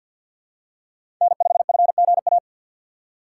Morse code, heard as a keyed tone, about 48 wpm, 700 Hz. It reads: N5FGA
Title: USA Call Signs - 5 Characters - 3x Character Spacing - 50wpm